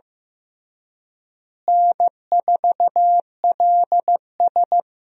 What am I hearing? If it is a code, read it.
Morse code, heard as a keyed tone, about 15 wpm, 700 Hz. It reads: N4LS
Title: USA Call Signs - 4 Characters - Repeated 3x - 15wpm